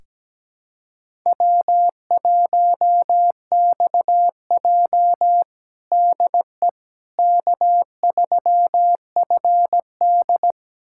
Morse code, heard as a keyed tone, about 17 wpm, 700 Hz. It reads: W1XJ DE K3FD
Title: USA Callsign DE Callsign - 4 Characters - 17wpm